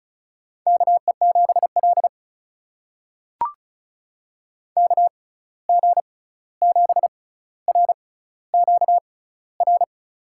Morse code, heard as a keyed tone, about 36 wpm, spaced as for 16 wpm, 700 Hz. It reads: KE7L KG7RQR
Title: USA Call Signs - 6x Character Spacing - 35wpm